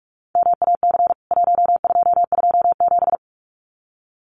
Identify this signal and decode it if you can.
Morse code, heard as a keyed tone, about 44 wpm, 700 Hz. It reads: MAC 1227
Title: CWT Exchange 45wpm